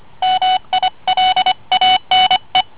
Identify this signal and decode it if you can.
Morse code, heard as a keyed tone, about 25 wpm, 750 Hz. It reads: MILANE